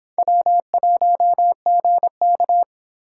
Morse code, heard as a keyed tone, about 30 wpm, 700 Hz. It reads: W1GK